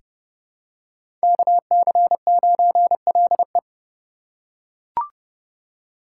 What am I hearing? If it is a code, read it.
Morse code, heard as a keyed tone, about 30 wpm, 700 Hz. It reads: KC9LE
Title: USA Call Signs - 5 Characters - 2x Character Spacing - 30wpm